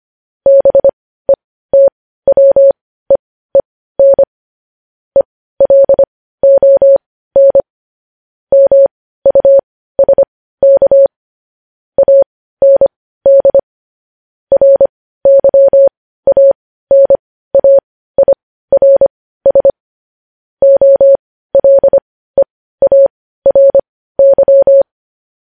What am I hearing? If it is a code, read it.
Morse code, heard as a keyed tone, about 25 wpm, spaced as for 15 wpm, 550 Hz. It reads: BETWEEN ELON MUSK AND RYANAIRS OLEARY
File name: podcast / News headlines-1x-25x15